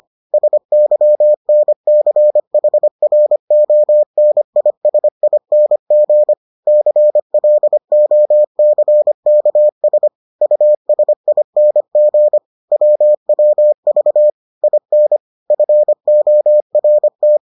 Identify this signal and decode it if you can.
Morse code, heard as a keyed tone, about 25 wpm, 600 Hz. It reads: SYNCHRONISING CLOCKS USING WWV IN FORT